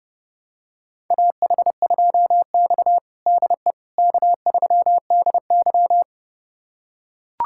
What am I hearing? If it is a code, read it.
Morse code, heard as a keyed tone, about 30 wpm, 700 Hz. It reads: AH2X DE K3DY
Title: USA Callsign DE Callsign - 4 Characters - 30wpm